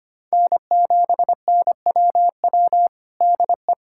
Morse code, heard as a keyed tone, about 25 wpm, 700 Hz. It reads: N7NWW DE
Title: Practice QSO - Exchange 4 - Ending - 25wpm